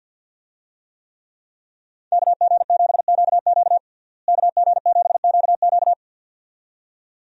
Morse code, heard as a keyed tone, about 50 wpm, 700 Hz. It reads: KG7QQ KG7QQ
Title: USA Call Signs - 5 Characters - Repeated 2x - 50wpm